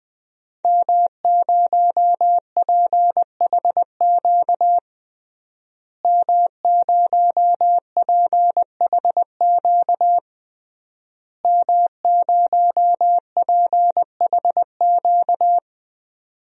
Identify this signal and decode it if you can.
Morse code, heard as a keyed tone, about 20 wpm, 700 Hz. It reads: M0PHQ M0PHQ M0PHQ
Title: DXCC-100 Call Signs - 5 Characters - Repeated 3x - 20wpm